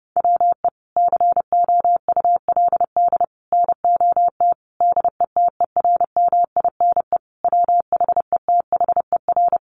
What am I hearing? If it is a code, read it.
Morse code, heard as a keyed tone, about 30 wpm, 700 Hz. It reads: WE COULD NOT DETERMINE WHETHER
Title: Sentences from Top 1300 Words 30wpm